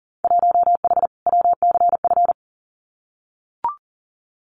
Morse code, heard as a keyed tone, about 40 wpm, 700 Hz. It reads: 1H WCF
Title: Winter Field Day Exchanges Encore 40wpm